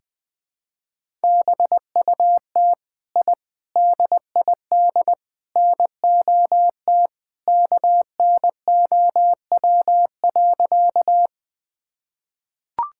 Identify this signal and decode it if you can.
Morse code, heard as a keyed tone, about 21 wpm, 700 Hz. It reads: BUT I DID NOT KNOW.
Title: Sentences with 4-Word Phrases - Spoken Beginning - 20wpm